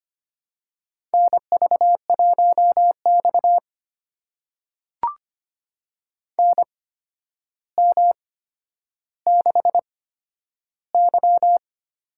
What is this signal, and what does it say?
Morse code, heard as a keyed tone, about 25 wpm, spaced as for 11 wpm, 700 Hz. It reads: NV1X NM6Y
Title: USA Call Signs - 4 Characters - 8x Character Spacing - 25wpm